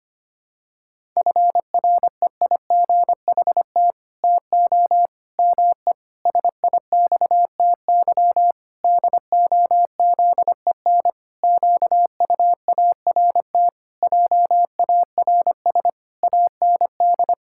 Morse code, heard as a keyed tone, about 25 wpm, 700 Hz. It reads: FREIGHT TO ME SIXTY DOZEN QUART JARS AND
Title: Pangram Challenge v2 - 25wpm